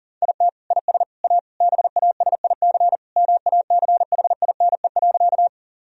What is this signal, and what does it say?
Morse code, heard as a keyed tone, about 37 wpm, 700 Hz. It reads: IT IS A BASIC MACHINE.
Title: Sentences from Top 1000 Words - Review Part 3 - 40wpm